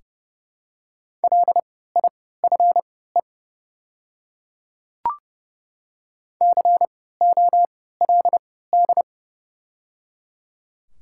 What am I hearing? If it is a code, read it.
Morse code, heard as a keyed tone, about 30 wpm, spaced as for 17 wpm, 700 Hz. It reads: LIFE COLD
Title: 3x Character Spacing - Top 500 Words - 30wpm